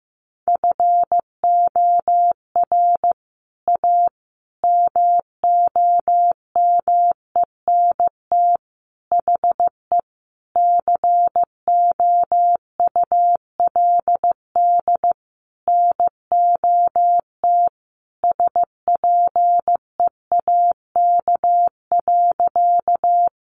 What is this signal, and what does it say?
Morse code, heard as a keyed tone, about 15 wpm, 700 Hz. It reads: FOR A MOMENT HE COULD NOT SPEAK.
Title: Sentences from Top 1100 Words 15wpm